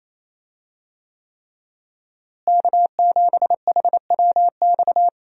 Morse code, heard as a keyed tone, about 28 wpm, 700 Hz. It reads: K7HWX
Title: USA Call Signs - 5 Characters - Repeated 3x - 28wpm